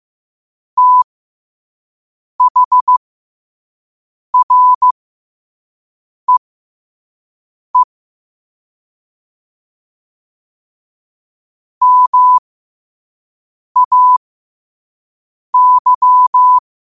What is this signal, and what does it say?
Morse code, heard as a keyed tone, about 15 wpm, spaced as for 5 wpm, 1000 Hz. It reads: THREE MAY
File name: qotdfiles / 5wpm